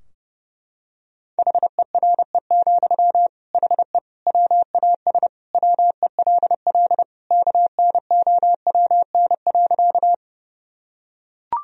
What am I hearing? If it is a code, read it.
Morse code, heard as a keyed tone, about 30 wpm, 700 Hz. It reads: HERE, HE WAS WELL KNOWN.
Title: Sentences from Top 1000 Words - Review Part 3 - 30wpm